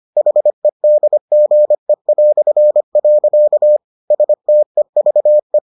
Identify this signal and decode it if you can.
Morse code, heard as a keyed tone, about 25 wpm, 600 Hz. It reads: HEDGE". STEVE